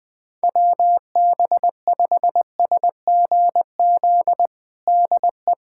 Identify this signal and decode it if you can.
Morse code, heard as a keyed tone, about 20 wpm, 700 Hz. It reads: WB5SGZ DE
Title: USA Callsign DE Callsign 20wpm